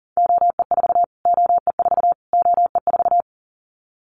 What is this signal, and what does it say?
Morse code, heard as a keyed tone, about 37 wpm, 700 Hz. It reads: OE4 OE4 OE4
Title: Callsign Prefixes for Top 25 DXCC - Repeated 3x - 40wpm